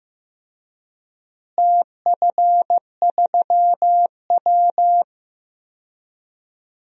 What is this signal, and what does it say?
Morse code, heard as a keyed tone, about 15 wpm, 700 Hz. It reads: TF3W
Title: DXCC-100 Call Signs - 4 Characters - Repeated 3x - 15wpm